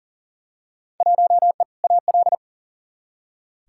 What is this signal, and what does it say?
Morse code, heard as a keyed tone, about 40 wpm, 700 Hz. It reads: 1E AL
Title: Field Day Exchanges - 40wpm